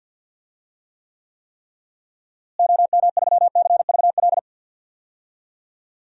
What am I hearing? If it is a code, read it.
Morse code, heard as a keyed tone, about 52 wpm, 700 Hz. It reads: OM3CVL
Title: DXCC-100 Call Signs - 6 Characters - Repeated 3x - 50wpm